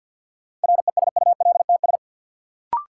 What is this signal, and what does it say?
Morse code, heard as a keyed tone, about 53 wpm, 700 Hz. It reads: RESULTS
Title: Sets of 4 Words - New Word Drill - 50wpm